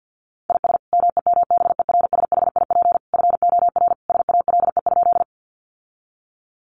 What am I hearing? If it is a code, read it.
Morse code, heard as a keyed tone, about 50 wpm, 700 Hz. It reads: IS MEMBERSHIP FOR SALE?
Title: Sentences from Top 1400 Words 50wpm